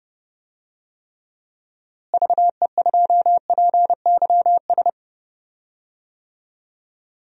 Morse code, heard as a keyed tone, about 30 wpm, 700 Hz. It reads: VE2PYS
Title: DXCC-100 Call Signs - 6 Characters - Repeated 3x - 30wpm